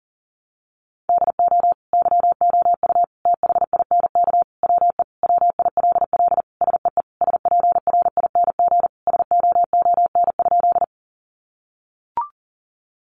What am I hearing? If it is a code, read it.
Morse code, heard as a keyed tone, about 40 wpm, 700 Hz. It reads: DO YOU THINK WE WILL SEE SPRING SOON?
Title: Sentences from Top 1200 Words 40wpm